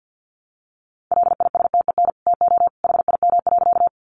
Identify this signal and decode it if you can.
Morse code, heard as a keyed tone, about 50 wpm, 700 Hz. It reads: LISTEN TO HIM.
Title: Sentences from Top 400 Words 50wpm